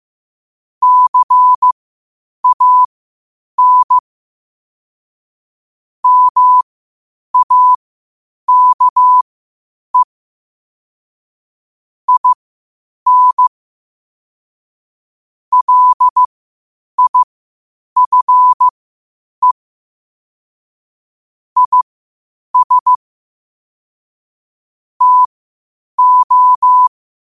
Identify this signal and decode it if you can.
Morse code, heard as a keyed tone, about 15 wpm, spaced as for 8 wpm, 1000 Hz. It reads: CAN MAKE IN LIFE IS TO
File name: qotdfiles / 8wpm